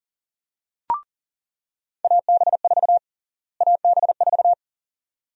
Morse code, heard as a keyed tone, about 40 wpm, 700 Hz. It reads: AB4 AB4
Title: USA Call Sign Prefixes - Repeated 2x - 40wpm